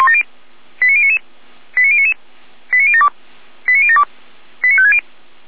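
Five-tone selective call (ZVEI), heard as quick runs of five tones, tones of about 70 ms. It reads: 89099 89099 89072 89072 89560
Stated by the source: Five-tone ZVEI selective call received by SDR, archived by feuerwehr-orlamuende.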